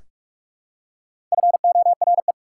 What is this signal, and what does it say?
Morse code, heard as a keyed tone, about 44 wpm, 690 Hz. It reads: FORE
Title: Prefixes and Suffixes 45wpm